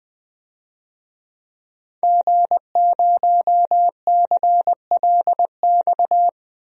Morse code, heard as a keyed tone, about 20 wpm, 700 Hz. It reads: G0CLX
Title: DXCC-100 Call Signs - 5 Characters - 20wpm